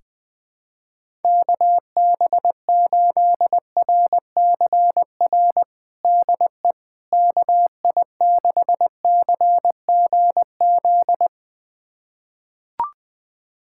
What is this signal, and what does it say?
Morse code, heard as a keyed tone, about 20 wpm, 700 Hz. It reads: KB8RCR DE KI6CGZ
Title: USA Callsign DE Callsign - 6 Characters - 20wpm